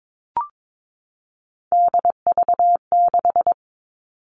Morse code, heard as a keyed tone, about 22 wpm, 700 Hz. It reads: DV6
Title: All Cut Numbers - 3-Digit 22wpm